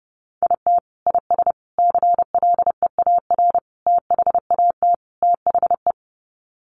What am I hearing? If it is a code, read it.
Morse code, heard as a keyed tone, about 30 wpm, 700 Hz. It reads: IT IS CLEAR THAT THE
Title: Sets of 5 Words Encore 30wpm